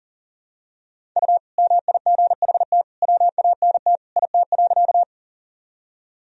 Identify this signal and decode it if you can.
Morse code, heard as a keyed tone, about 40 wpm, 700 Hz. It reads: U MIGHT WANT IT.
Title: Sentences from Top 300 Words and Abbreviations 40wpm